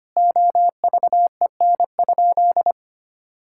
Morse code, heard as a keyed tone, about 25 wpm, 700 Hz. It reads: OVEN?
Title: Sentences from Top 1000 Words - Review Part 6 - 25wpm